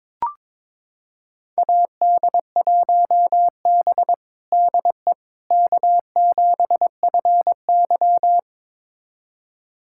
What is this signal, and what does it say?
Morse code, heard as a keyed tone, about 22 wpm, 700 Hz. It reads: AD1B DE K7FY
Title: USA Callsign DE Callsign - 4 Characters - 22wpm